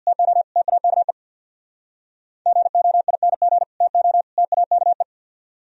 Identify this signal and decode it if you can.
Morse code, heard as a keyed tone, about 65 wpm, 700 Hz. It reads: TO TAKE GOING TO TAKE